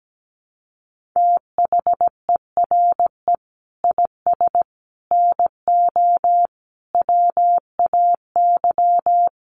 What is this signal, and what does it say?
Morse code, heard as a keyed tone, about 17 wpm, 700 Hz. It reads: THERE IS NO WAY
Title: Sets of 5 Words Encore 17wpm